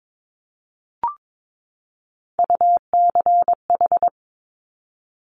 Morse code, heard as a keyed tone, about 22 wpm, 700 Hz. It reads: UCH
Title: Top Three Letter Combinations in Words 22wpm